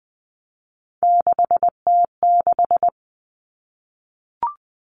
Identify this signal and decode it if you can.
Morse code, heard as a keyed tone, about 20 wpm, 700 Hz. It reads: 6T6
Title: All Cut Numbers - 3-Digit 20wpm